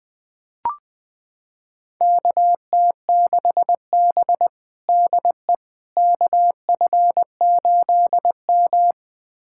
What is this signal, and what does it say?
Morse code, heard as a keyed tone, about 21 wpm, 700 Hz. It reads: KT6B DE KF8M
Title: USA Callsign DE Callsign - 4 Characters - 20wpm